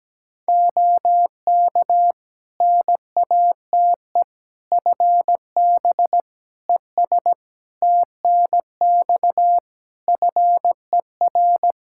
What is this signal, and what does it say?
Morse code, heard as a keyed tone, about 17 wpm, 700 Hz. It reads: OK NATE FB ES TNX FER
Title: Practice QSO - Exchange 2 - Info - 17wpm